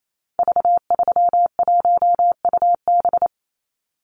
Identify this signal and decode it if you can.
Morse code, heard as a keyed tone, about 28 wpm, 700 Hz. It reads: V31UB
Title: Call Signs - 28wpm